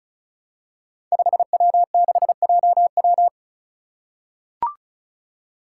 Morse code, heard as a keyed tone, about 36 wpm, 700 Hz. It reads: 5W6JW
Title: Five Challenging Letter-Number Combinations 35wpm